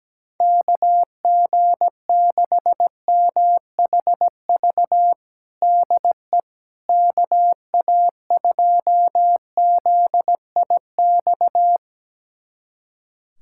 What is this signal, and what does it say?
Morse code, heard as a keyed tone, about 17 wpm, 700 Hz. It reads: KG6MHV DE KA2ZIX
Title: USA Callsign DE Callsign 17wpm